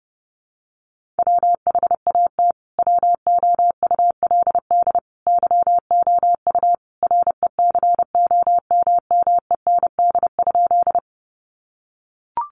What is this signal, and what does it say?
Morse code, heard as a keyed tone, about 30 wpm, 700 Hz. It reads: WHAT WOULD YOU RECOMMEND?
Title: Sentences from Top 1200 Words 30wpm